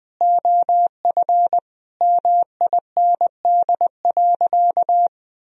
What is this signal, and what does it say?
Morse code, heard as a keyed tone, about 21 wpm, 700 Hz. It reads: OF MIND.
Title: Sentences from Top 1000 Words - Review Part 2 - 20wpm